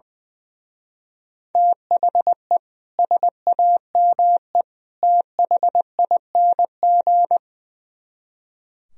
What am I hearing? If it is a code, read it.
Morse code, heard as a keyed tone, about 22 wpm, 700 Hz. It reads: THE SAME THING
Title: Speed-Racing - Sets of 3 Words 20wpm